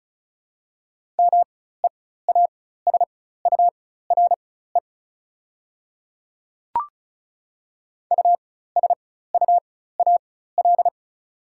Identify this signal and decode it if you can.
Morse code, heard as a keyed tone, about 36 wpm, spaced as for 16 wpm, 690 Hz. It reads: MEASURE USUAL
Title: 4x Character Spacing - Top 500 Words - 35wpm